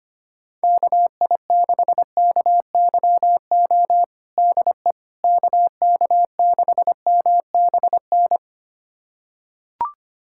Morse code, heard as a keyed tone, about 25 wpm, 700 Hz. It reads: KI6KYO DE KK6MBN
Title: USA Callsign DE Callsign - 6 Characters - 25wpm